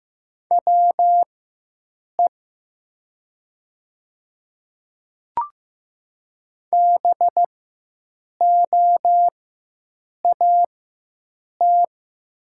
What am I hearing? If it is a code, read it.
Morse code, heard as a keyed tone, about 15 wpm, spaced as for 7 wpm, 700 Hz. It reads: WE BOAT
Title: 4x Character Spacing - Top 500 Words - 15wpm